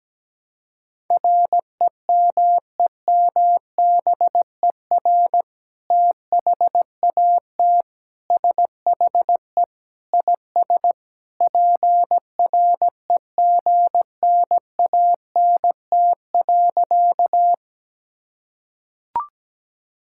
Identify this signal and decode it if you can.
Morse code, heard as a keyed tone, about 17 wpm, 700 Hz. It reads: REMEMBER THAT SHE IS PREGNANT.
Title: Sentences from Top 1000 Words - Review Part 6 - 17wpm